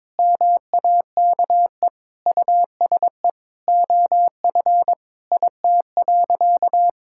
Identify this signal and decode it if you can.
Morse code, heard as a keyed tone, about 21 wpm, 700 Hz. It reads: MAKE USE OF IT.
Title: Sentences from Top 1000 Words - Review Part 6 - 22wpm